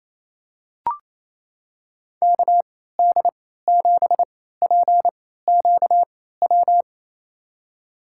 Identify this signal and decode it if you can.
Morse code, heard as a keyed tone, about 28 wpm, spaced as for 16 wpm, 700 Hz. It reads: KD7PQW
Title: USA Call Signs - 6 Characters - 3x Character Spacing - 28wpm